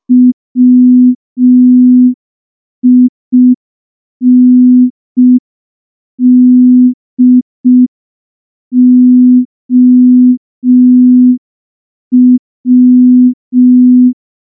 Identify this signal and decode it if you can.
Morse code, heard as a keyed tone, about 5 wpm, 250 Hz. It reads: WINDOW
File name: morse / morse-training-05WPM